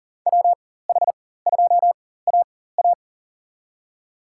Morse code, heard as a keyed tone, about 40 wpm, spaced as for 19 wpm, 700 Hz. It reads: WH2AA
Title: USA Call Signs - 5 Characters - 4x Character Spacing - 40wpm